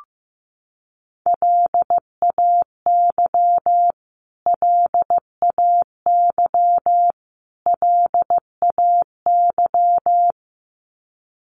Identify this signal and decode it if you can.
Morse code, heard as a keyed tone, about 15 wpm, 700 Hz. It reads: LAY LAY LAY
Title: Top 500 Words Repeated 15wpm